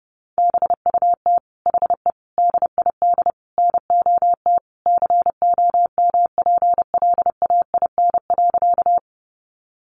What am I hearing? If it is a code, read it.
Morse code, heard as a keyed tone, about 30 wpm, 700 Hz. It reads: BUT HE DID NOT COMPLAIN.
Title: Sentences from Top 1300 Words 30wpm